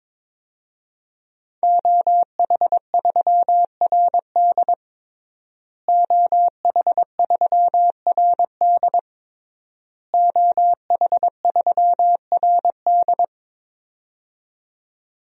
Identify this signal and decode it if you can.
Morse code, heard as a keyed tone, about 23 wpm, 700 Hz. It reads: OH3RD OH3RD OH3RD
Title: DXCC-100 Call Signs - 5 Characters - Repeated 3x - 22wpm